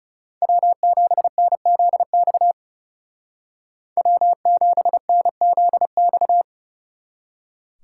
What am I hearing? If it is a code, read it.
Morse code, heard as a keyed tone, about 37 wpm, 700 Hz. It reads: W7NZX W7NZX
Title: Speed Racing - USA Call Signs 30wpm